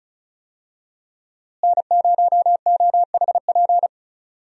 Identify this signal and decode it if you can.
Morse code, heard as a keyed tone, about 36 wpm, 700 Hz. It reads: N0OHP
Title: USA Call Signs - 5 Characters - Repeated 3x - 35wpm